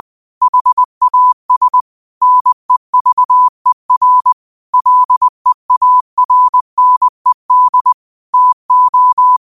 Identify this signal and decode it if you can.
Morse code, heard as a keyed tone, about 20 wpm, 1000 Hz. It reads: HAS NEVER LEARNED TO
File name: qotdfiles / 20wpm